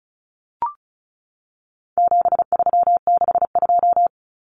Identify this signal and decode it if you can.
Morse code, heard as a keyed tone, about 34 wpm, 700 Hz. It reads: 7362
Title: Numbers - 4-Digit 35wpm